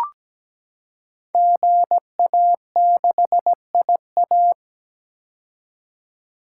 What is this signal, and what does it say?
Morse code, heard as a keyed tone, about 17 wpm, 700 Hz. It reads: GA6IA
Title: Code Groups - 5 Letter:Numbers - 17wpm